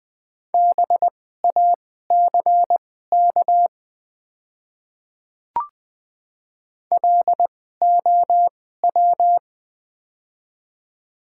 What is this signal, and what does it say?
Morse code, heard as a keyed tone, about 20 wpm, spaced as for 15 wpm, 700 Hz. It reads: BACK LOW
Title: 2x Character Spacing - Top 300 Words - 20wpm